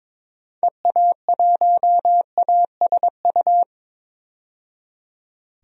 Morse code, heard as a keyed tone, about 22 wpm, 700 Hz. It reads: EA1ASU
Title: DXCC-100 Call Signs - 6 Characters - 22wpm